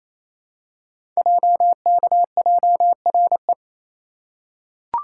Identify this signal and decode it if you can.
Morse code, heard as a keyed tone, about 29 wpm, 700 Hz. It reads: JKJRE
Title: Code Groups - 5 Letters - 28wpm